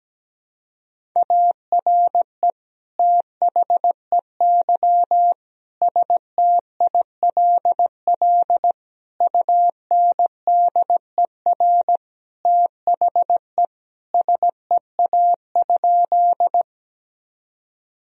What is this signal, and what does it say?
Morse code, heard as a keyed tone, about 17 wpm, 700 Hz. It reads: ARE THEY STILL UNDER THE SEA?
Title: Sentences from Top 1000 Words - Review Part 1 - 17wpm